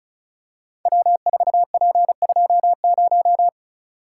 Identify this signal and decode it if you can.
Morse code, heard as a keyed tone, about 36 wpm, 690 Hz. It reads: W4P20
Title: Code Groups - 5 Letter:Numbers - 35wpm